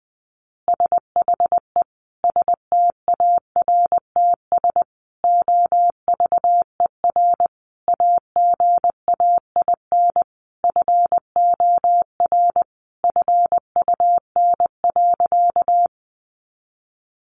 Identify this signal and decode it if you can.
Morse code, heard as a keyed tone, about 20 wpm, 700 Hz. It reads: SHE STARTS OVER AGAIN FOR FUN.
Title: Easy Phrases 20wpm